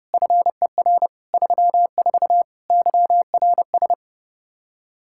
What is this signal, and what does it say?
Morse code, heard as a keyed tone, about 30 wpm, 700 Hz. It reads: FER 34 YRS
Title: Practice QSO - Exchange 3 - More Info - 30wpm